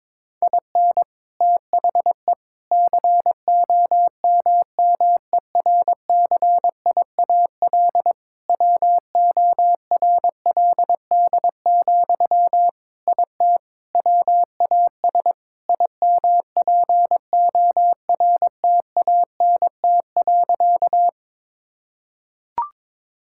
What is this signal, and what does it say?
Morse code, heard as a keyed tone, about 21 wpm, 700 Hz. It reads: IN THE COMMERCIAL WORLD, IT WAS IMPORTANT.
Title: Sentences from Top 1000 Words - Review Part 5 - 22wpm